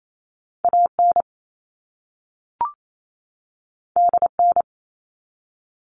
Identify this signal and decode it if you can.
Morse code, heard as a keyed tone, about 28 wpm, 700 Hz. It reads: AN DN
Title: All Cut Numbers - 2-Digit 28wpm